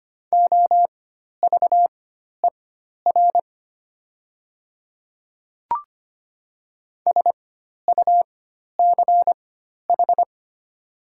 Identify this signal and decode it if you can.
Morse code, heard as a keyed tone, about 25 wpm, spaced as for 12 wpm, 700 Hz. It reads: OVER SUCH
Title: 4x Character Spacing - Top 300 Words - 25wpm